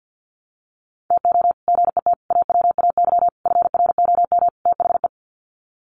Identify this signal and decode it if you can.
Morse code, heard as a keyed tone, about 50 wpm, 700 Hz. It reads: TO GET AWAY FROM THE
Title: Sets of 5 Words Encore2 50wpm